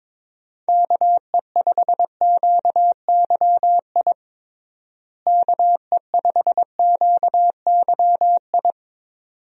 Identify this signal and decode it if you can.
Morse code, heard as a keyed tone, about 22 wpm, 700 Hz. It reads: KE5QYI KE5QYI
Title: USA Call Signs - 6 Characters - Repeated 2x - 22wpm